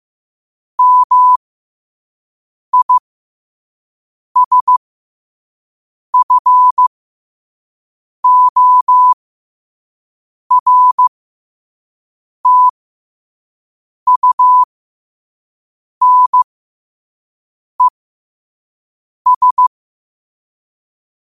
Morse code, heard as a keyed tone, about 15 wpm, spaced as for 5 wpm, 1000 Hz. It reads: MISFORTUNES
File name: qotdfiles / 5wpm